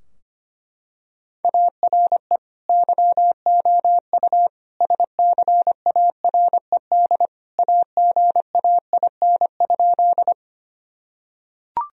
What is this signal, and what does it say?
Morse code, heard as a keyed tone, about 25 wpm, 700 Hz. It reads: ARE YOU SCARED AGAIN?
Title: Sentences from Top 1000 Words - Review Part 4 - 25wpm